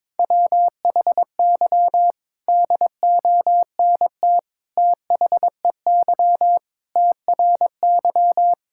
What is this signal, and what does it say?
Morse code, heard as a keyed tone, about 22 wpm, 700 Hz. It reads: WHY DONT THEY TRY
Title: Sentences from Top 1000 Words - Review Part 5 - 22wpm